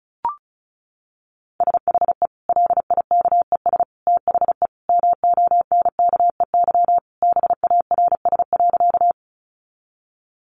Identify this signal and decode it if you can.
Morse code, heard as a keyed tone, about 34 wpm, 700 Hz. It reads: SHE LIKES THE MONKEY BARS.
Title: Easy Phrases 35wpm